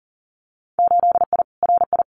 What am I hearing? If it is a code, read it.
Morse code, heard as a keyed tone, about 40 wpm, 700 Hz. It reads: 8I RI
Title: Winter Field Day Exchanges Encore 40wpm